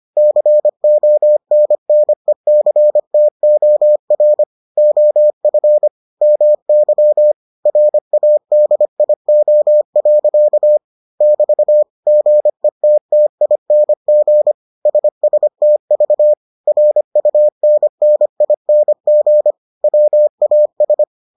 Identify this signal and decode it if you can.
Morse code, heard as a keyed tone, about 25 wpm, 600 Hz. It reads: CONNECTOR OF MY RADIO. <BT> GETTING SSTV RUNNING WAS